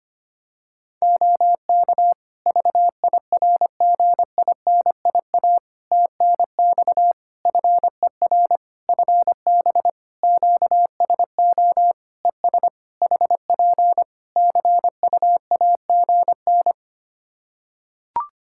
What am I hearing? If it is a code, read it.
Morse code, heard as a keyed tone, about 25 wpm, 700 Hz. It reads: OK VIRGINIA TNX FER FB QSO ES HP CUAGN
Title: Practice QSO - Exchange 4 - Ending - 25wpm